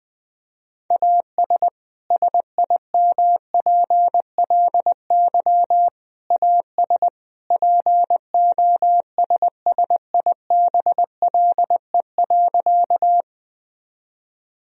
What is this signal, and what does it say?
Morse code, heard as a keyed tone, about 20 wpm, 700 Hz. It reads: AS SIMPLY AS POSSIBLE.
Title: Sentences from Top 1000 Words - Review Part 4 - 20wpm